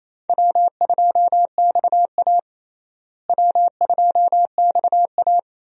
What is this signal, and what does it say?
Morse code, heard as a keyed tone, about 28 wpm, 700 Hz. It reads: W2XA W2XA
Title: USA Call Signs - 4 Characters - Repeated 2x - 28wpm